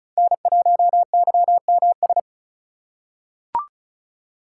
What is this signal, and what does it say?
Morse code, heard as a keyed tone, about 36 wpm, 690 Hz. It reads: N1YMS
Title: USA Call Signs - 5 Characters - 3x Character Spacing - 35wpm